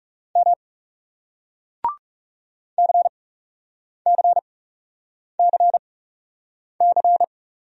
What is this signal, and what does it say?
Morse code, heard as a keyed tone, about 38 wpm, spaced as for 8 wpm, 700 Hz. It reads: M CCCC